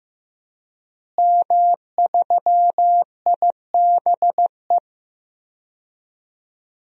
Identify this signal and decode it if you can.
Morse code, heard as a keyed tone, about 16 wpm, 700 Hz. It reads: M3IBE
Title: DXCC-100 Call Signs - 5 Characters - Repeated 3x - 15wpm